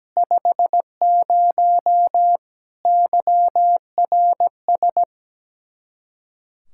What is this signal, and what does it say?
Morse code, heard as a keyed tone, about 17 wpm, 700 Hz. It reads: 50 YRS
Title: Practice QSO - Exchange 3 - More Info - 17wpm